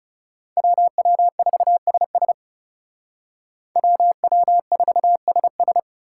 Morse code, heard as a keyed tone, about 38 wpm, 700 Hz. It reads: WW4SS WW4SS